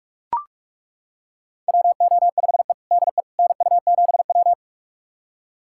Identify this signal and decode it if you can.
Morse code, heard as a keyed tone, about 42 wpm, 700 Hz. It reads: WO5E DE NU7W
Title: USA Callsign DE Callsign - 4 Characters - 45wpm